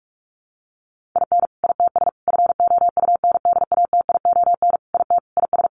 Morse code, heard as a keyed tone, about 44 wpm, 700 Hz. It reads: IN ITS FOUNDATION IT IS
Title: Sentences from Top 1200 Words 45wpm